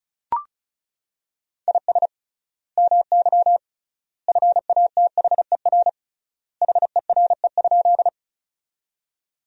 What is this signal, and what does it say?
Morse code, heard as a keyed tone, about 36 wpm, 700 Hz. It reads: IS MY FATHER HERE?
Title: 3x Word-Spacing - Sentences from Top 300 Words - 35wpm